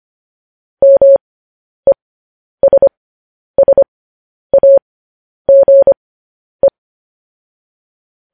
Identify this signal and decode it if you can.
Morse code, heard as a keyed tone, about 25 wpm, spaced as for 10 wpm, 550 Hz. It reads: MESSAGE
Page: News Headlines in Morse Code at 10 WPM